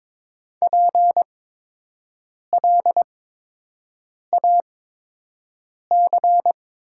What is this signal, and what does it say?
Morse code, heard as a keyed tone, about 22 wpm, spaced as for 6 wpm, 700 Hz. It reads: PLAC
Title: 8x Character Spacing - Top 200 Words - 22wpm